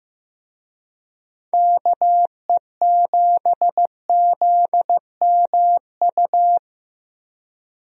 Morse code, heard as a keyed tone, about 15 wpm, 700 Hz. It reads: KE7ZMU
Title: USA Call Signs - 6 Characters - 15wpm